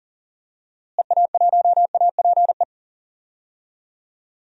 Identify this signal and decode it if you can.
Morse code, heard as a keyed tone, about 39 wpm, 700 Hz. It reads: EA1APE
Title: DXCC-100 Call Signs - 6 Characters - Repeated 2x - 40wpm